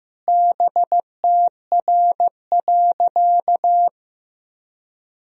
Morse code, heard as a keyed tone, about 15 wpm, 700 Hz. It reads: BTR.
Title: Sentences from Top 500 Words and Abbreviations 15wpm